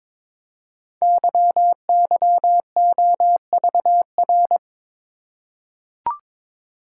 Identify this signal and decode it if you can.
Morse code, heard as a keyed tone, about 22 wpm, 700 Hz. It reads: YYOVR
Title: Code Groups - 5 Letters - 22wpm